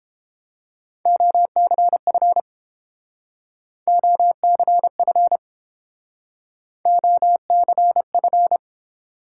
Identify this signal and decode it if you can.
Morse code, heard as a keyed tone, about 30 wpm, 700 Hz. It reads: OCF OCF OCF